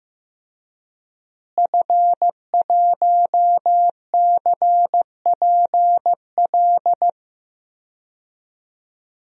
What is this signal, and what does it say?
Morse code, heard as a keyed tone, about 16 wpm, 700 Hz. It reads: F1CPL
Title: DXCC-100 Call Signs - 5 Characters - Repeated 2x - 15wpm